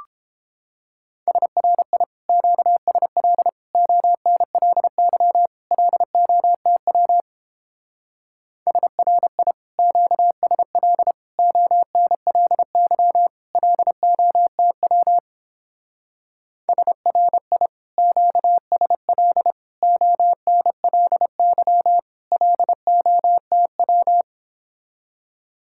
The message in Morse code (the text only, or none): SRI QSL ONLY LOTW SRI QSL ONLY LOTW SRI QSL ONLY LOTW